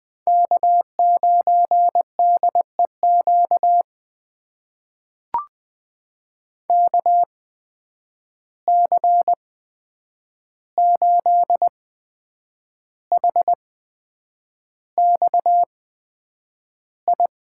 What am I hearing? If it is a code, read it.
Morse code, heard as a keyed tone, about 20 wpm, spaced as for 8 wpm, 700 Hz. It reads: K9DEQ KC8HXI
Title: USA Call Signs - 8x Character Spacing - 20wpm